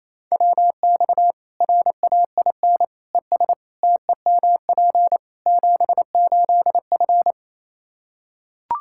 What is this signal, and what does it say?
Morse code, heard as a keyed tone, about 28 wpm, 700 Hz. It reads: WX RAIN ES TEMP 78F
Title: K1USN SST QSO Practice 28wpm